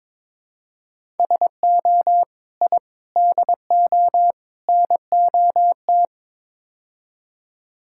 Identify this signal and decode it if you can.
Morse code, heard as a keyed tone, about 22 wpm, 700 Hz. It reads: SO I DO NOT
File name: practice / 22wpm / Build A Phrase - 5-Word Phrases - 22wpm